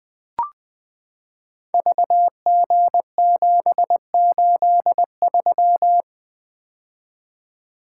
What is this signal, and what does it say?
Morse code, heard as a keyed tone, about 20 wpm, 700 Hz. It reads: VG783
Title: Code Groups - 5 Letter:Numbers - 20wpm